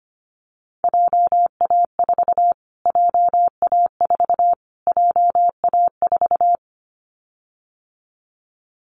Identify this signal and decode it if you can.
Morse code, heard as a keyed tone, about 25 wpm, 700 Hz. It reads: JA4 JA4 JA4
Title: Callsign Prefixes for Top 25 DXCC - Repeated 3x - 25wpm